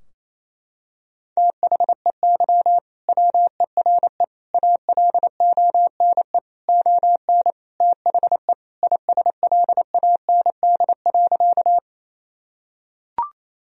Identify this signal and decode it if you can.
Morse code, heard as a keyed tone, about 28 wpm, 700 Hz. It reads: THEY WERE ALONE ON THE ISLAND.
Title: Sentences from Top 1000 Words - Review Part 3 - 28wpm